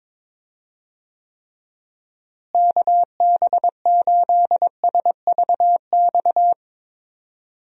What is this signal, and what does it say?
Morse code, heard as a keyed tone, about 22 wpm, 700 Hz. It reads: KB8SVX
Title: USA Call Signs - 6 Characters - Repeated 2x - 22wpm